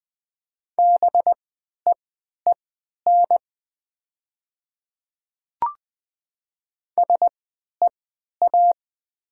Morse code, heard as a keyed tone, about 20 wpm, spaced as for 11 wpm, 690 Hz. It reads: BEEN SEA
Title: 3x Character Spacing - Top 300 Words - 20wpm